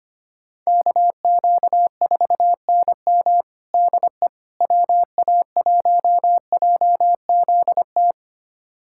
Morse code, heard as a keyed tone, about 25 wpm, 700 Hz. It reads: KQ4NM DE WA1JZT
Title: USA Callsign DE Callsign 25wpm